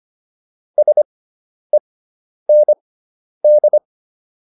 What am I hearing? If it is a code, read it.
Morse code, heard as a keyed tone, about 25 wpm, spaced as for 10 wpm, 600 Hz. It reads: SEND